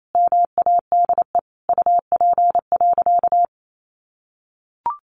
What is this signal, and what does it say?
Morse code, heard as a keyed tone, about 28 wpm, 700 Hz. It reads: MADE UP.
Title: Sentences from Top 500 Words 28wpm